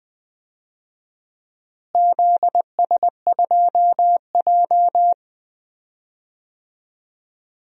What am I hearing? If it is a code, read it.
Morse code, heard as a keyed tone, about 20 wpm, 700 Hz. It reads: ZS2J
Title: DXCC-100 Call Signs - 4 Characters - Repeated 3x - 20wpm